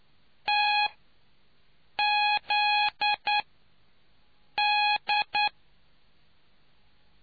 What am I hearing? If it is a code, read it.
Morse code, heard as a keyed tone, about 9 wpm, spaced as for 5 wpm, 830 Hz. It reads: TZD